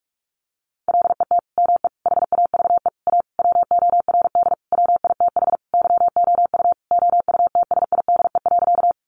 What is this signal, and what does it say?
Morse code, heard as a keyed tone, about 44 wpm, 700 Hz. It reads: LET ME HAVE A WORD WITH YOU OUTSIDE.
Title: Sentences from Top 900 Words 45wpm